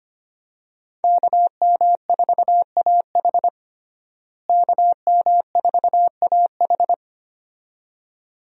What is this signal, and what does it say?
Morse code, heard as a keyed tone, about 25 wpm, 700 Hz. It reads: KM4AH KM4AH
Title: USA Call Signs - 5 Characters - Repeated 2x - 25wpm